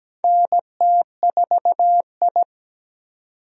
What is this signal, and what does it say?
Morse code, heard as a keyed tone, about 17 wpm, 700 Hz. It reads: NT4I
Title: USA Call Signs - 4 Characters - 17wpm